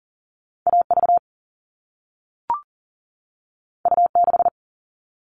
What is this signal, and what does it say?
Morse code, heard as a keyed tone, about 42 wpm, 700 Hz. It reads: AV U6
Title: All Cut Numbers - 2-Digit 40wpm